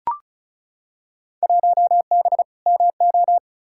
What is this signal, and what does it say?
Morse code, heard as a keyed tone, about 36 wpm, 700 Hz. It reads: 1B MO
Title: Field Day Exchanges - 35wpm